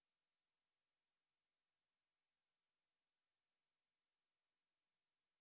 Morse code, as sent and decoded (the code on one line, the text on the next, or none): none